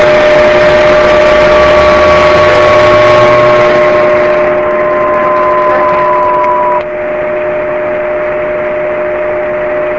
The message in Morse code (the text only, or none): none